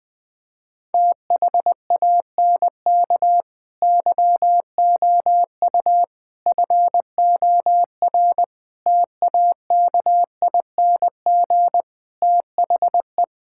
Morse code, heard as a keyed tone, about 21 wpm, 700 Hz. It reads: THANK YOU FOR TAKING THE
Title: Sets of 9 Words - 20wpm